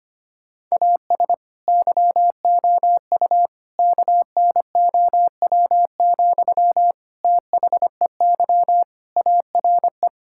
Morse code, heard as a keyed tone, about 25 wpm, 700 Hz. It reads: AS YOU KNOW, THEY ARE
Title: Sentences from Top 1000 Words - Review Part 5 - 25wpm